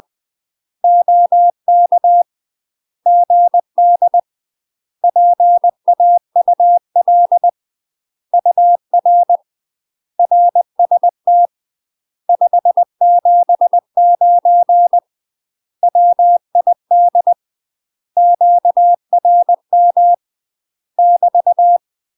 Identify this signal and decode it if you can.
Morse code, heard as a keyed tone, about 20 wpm, 700 Hz. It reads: OK GD PAUL UR RST 579 WID QRM <BT>
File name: UK MASTER QSOs TRAINING / 20WPM / UK MASTER QSOs 0036